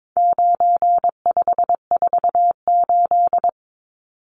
Morse code, heard as a keyed tone, about 22 wpm, 700 Hz. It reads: 9548
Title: Numbers - 4-Digit 22wpm